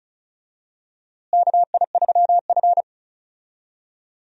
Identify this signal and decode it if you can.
Morse code, heard as a keyed tone, about 36 wpm, 700 Hz. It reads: KI3F
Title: USA Call Signs - 4 Characters - Repeated 2x - 35wpm